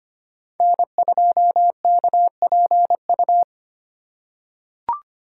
Morse code, heard as a keyed tone, about 25 wpm, 700 Hz. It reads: N2KPU
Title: USA Call Signs - 5 Characters - 4x Character Spacing - 25wpm